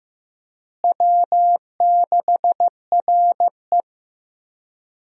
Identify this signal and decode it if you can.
Morse code, heard as a keyed tone, about 15 wpm, 700 Hz. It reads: W6RE
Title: USA Call Signs - 4 Characters - Repeated 3x - 15wpm